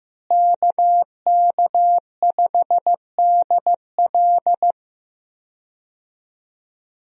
Morse code, heard as a keyed tone, about 15 wpm, 700 Hz. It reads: KK5DL
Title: USA Call Signs - 5 Characters - Repeated 3x - 15wpm